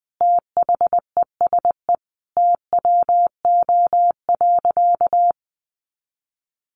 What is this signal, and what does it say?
Morse code, heard as a keyed tone, about 19 wpm, 700 Hz. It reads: THESE TWO.
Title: Sentences from Top 1200 Words 20wpm